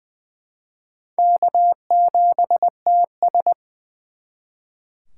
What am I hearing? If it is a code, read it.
Morse code, heard as a keyed tone, about 24 wpm, 700 Hz. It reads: K7TS
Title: Speed Racing - USA 1x2 Call Signs 20wpm